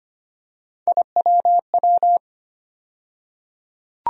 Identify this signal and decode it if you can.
Morse code, heard as a keyed tone, about 25 wpm, 700 Hz. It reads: IWW
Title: USA Call Sign Postfixes - 3 Letters 25wpm